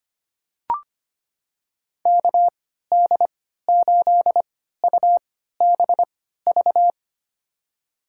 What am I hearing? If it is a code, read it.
Morse code, heard as a keyed tone, about 25 wpm, spaced as for 14 wpm, 700 Hz. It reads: KD8UBV
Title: USA Call Signs - 6 Characters - 3x Character Spacing - 25wpm